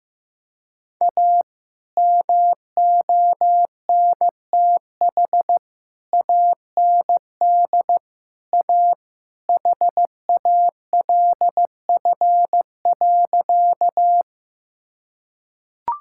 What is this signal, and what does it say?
Morse code, heard as a keyed tone, about 15 wpm, 700 Hz. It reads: A MONTH AND A HALF.
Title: Sentences from Top 1000 Words - Review Part 5 - 15wpm